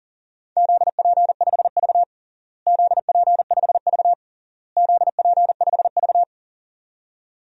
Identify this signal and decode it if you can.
Morse code, heard as a keyed tone, about 40 wpm, 700 Hz. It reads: ZP5V ZP5V ZP5V
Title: DXCC-100 Call Signs - 4 Characters - Repeated 3x - 40wpm